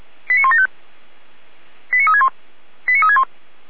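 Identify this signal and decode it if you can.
Five-tone selective call (ZVEI), heard as quick runs of five tones, tones of about 70 ms. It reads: 89175 89361 89361
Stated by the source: ZVEI five-tone call received by SDR.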